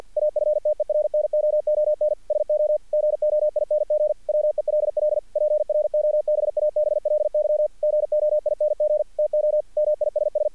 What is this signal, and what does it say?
Morse code, heard as a keyed tone, about 50 wpm, 600 Hz. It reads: AFTERNOON SO GOING WELL PROBABLY GOING TO MISS
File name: archived / CRSnet